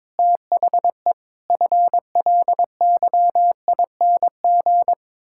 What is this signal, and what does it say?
Morse code, heard as a keyed tone, about 22 wpm, 700 Hz. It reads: THE FLYING
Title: Pangram Challenge v2 - 22wpm